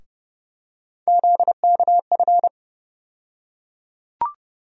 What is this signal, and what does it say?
Morse code, heard as a keyed tone, about 31 wpm, 700 Hz. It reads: ZKF
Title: USA Call Sign Postfixes - 3 Letters 30wpm